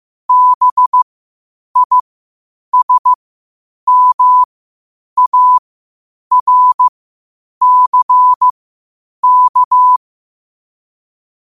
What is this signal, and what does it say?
Morse code, heard as a keyed tone, about 15 wpm, spaced as for 8 wpm, 1000 Hz. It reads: BISMARCK
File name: qotdfiles / 8wpm